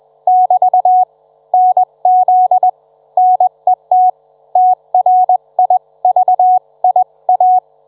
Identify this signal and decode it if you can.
Morse code, heard as a keyed tone, about 21 wpm, 740 Hz. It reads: <BT> NZ NET TRIVIA